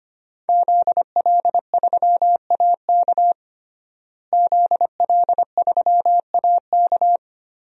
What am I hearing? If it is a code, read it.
Morse code, heard as a keyed tone, about 25 wpm, 700 Hz. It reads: ZL3AK ZL3AK
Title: DXCC-100 Call Signs - 5 Characters - Repeated 2x - 25wpm